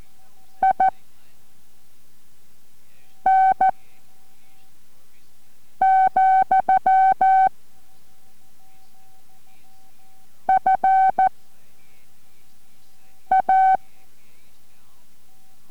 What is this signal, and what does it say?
Morse code, heard as a keyed tone, about 14 wpm, 760 Hz. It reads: I N , F A